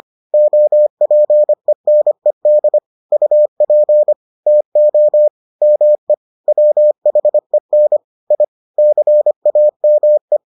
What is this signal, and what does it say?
Morse code, heard as a keyed tone, about 25 wpm, 600 Hz. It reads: OPENED UP TO ME WHEN I CAME